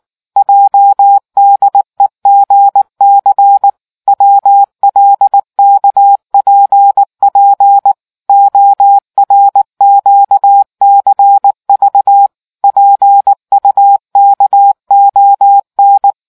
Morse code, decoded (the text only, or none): JDEGC WLKPP ORQCV PUKON